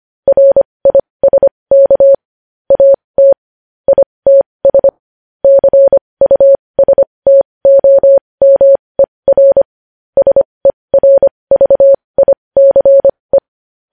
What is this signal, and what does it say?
Morse code, heard as a keyed tone, about 25 wpm, spaced as for 20 wpm, 550 Hz. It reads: RISK AT ITS CUSTOMER SERVICE
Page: news-headlines-in-morse-code-at-20-wpm